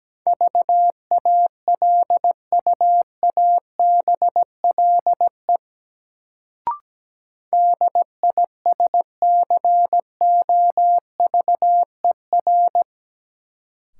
Morse code, heard as a keyed tone, about 17 wpm, 700 Hz. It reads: VALUABLE DISCOVER
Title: Sentences from Top 1100 Words - New Word Drill - 17wpm